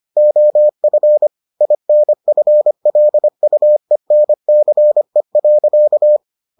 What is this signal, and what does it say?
Morse code, heard as a keyed tone, about 25 wpm, 600 Hz. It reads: OF INFLUENCE.